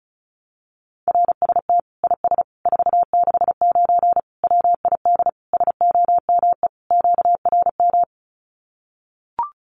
Morse code, heard as a keyed tone, about 34 wpm, 700 Hz. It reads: RST IS 469 WID SOME QRM
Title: Common QSO Phrases 35wpm